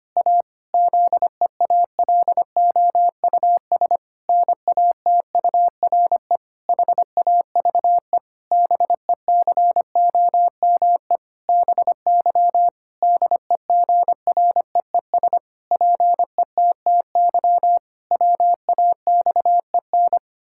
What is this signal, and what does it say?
Morse code, heard as a keyed tone, about 25 wpm, 700 Hz. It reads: A ZEALOUS NATURE HAVE BECOME BY DEGREES PETTY WAXEN
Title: Pangram Challenge v2 - 25wpm